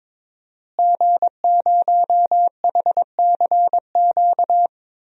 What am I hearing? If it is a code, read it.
Morse code, heard as a keyed tone, about 22 wpm, 700 Hz. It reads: G0HCQ
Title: DXCC-100 Call Signs - 5 Characters - Repeated 3x - 22wpm